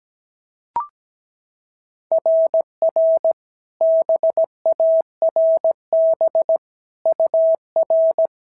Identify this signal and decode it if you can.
Morse code, heard as a keyed tone, about 16 wpm, 650 Hz. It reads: RR BARB UR
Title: SOTA QSO Practice 17wpm